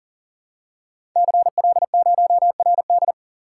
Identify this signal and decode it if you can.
Morse code, heard as a keyed tone, about 38 wpm, 690 Hz. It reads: CL0RD
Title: Code Groups - 5 Letter:Numbers - 40wpm